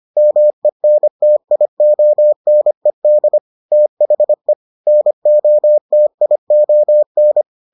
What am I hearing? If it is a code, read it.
Morse code, heard as a keyed tone, about 25 wpm, 600 Hz. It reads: MENTIONED THE NOTION